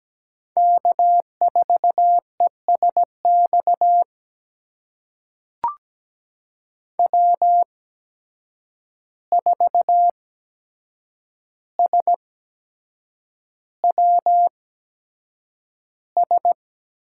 Morse code, heard as a keyed tone, about 17 wpm, spaced as for 7 wpm, 700 Hz. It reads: K4ESX W4SWS
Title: USA Call Signs - 5 Characters - 8x Character Spacing - 17wpm